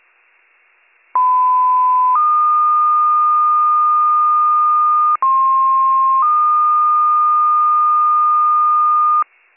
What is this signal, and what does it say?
Two-tone page, heard as a one-second tone, then a three-second tone, about 1010 then 1220 Hz, about 1060 then 1220 Hz. It is heard from 1.1 to 9.2 s.